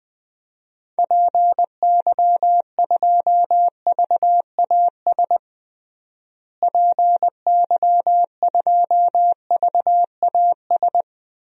Morse code, heard as a keyed tone, about 20 wpm, 700 Hz. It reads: PY2VAS PY2VAS
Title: DXCC-100 Call Signs - 6 Characters - Repeated 2x - 20wpm